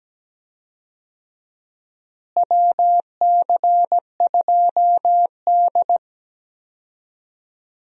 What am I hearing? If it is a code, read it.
Morse code, heard as a keyed tone, about 17 wpm, 700 Hz. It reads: WC2D
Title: USA Call Signs - 4 Characters - Repeated 3x - 17wpm